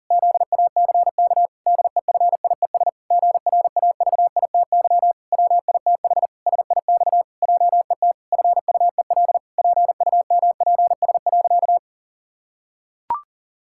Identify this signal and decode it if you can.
Morse code, heard as a keyed tone, about 40 wpm, 700 Hz. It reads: ZACK DEFIES GRAVITY WITH SIX JET FUEL PUMPS.
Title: Pangram Challenge v2 - 40wpm